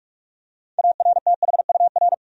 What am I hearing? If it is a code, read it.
Morse code, heard as a keyed tone, about 41 wpm, 700 Hz. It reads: ARTHUR
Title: USA Common Names - 45wpm